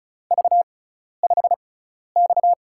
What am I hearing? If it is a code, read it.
Morse code, heard as a keyed tone, about 36 wpm, spaced as for 12 wpm, 700 Hz. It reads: V5X
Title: USA Call Signs - 6x Character Spacing - 35wpm